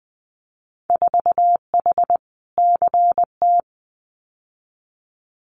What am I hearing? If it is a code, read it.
Morse code, heard as a keyed tone, about 20 wpm, 700 Hz. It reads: 4H CT
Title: Winter Field Day Exchanges Encore 20wpm